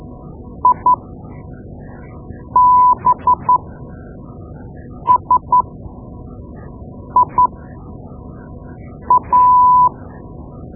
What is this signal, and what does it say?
Morse code, heard as a keyed tone, about 12 wpm, spaced as for 5 wpm, 1000 Hz. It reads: IBSIA
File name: Morse Lessons / Morse Lessons / Tape-2B